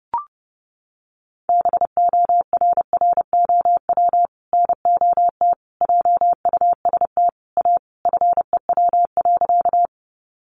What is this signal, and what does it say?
Morse code, heard as a keyed tone, about 30 wpm, 700 Hz. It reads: BORROW NOT JUST A FEW.
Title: Sentences from Top 1500 Words 30wpm